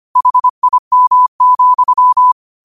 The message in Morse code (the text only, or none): HIM,